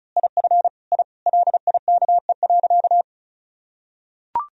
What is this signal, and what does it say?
Morse code, heard as a keyed tone, about 36 wpm, 700 Hz. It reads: IF I LIKE.
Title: Sentences from Top 1000 Words - Review Part 1 - 35wpm